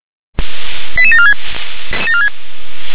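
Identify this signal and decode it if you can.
Five-tone selective call (ZVEI), heard as quick runs of five tones, tones of about 70 ms. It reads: 88746
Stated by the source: Five-tone ZVEI call 88914 received on an SDR.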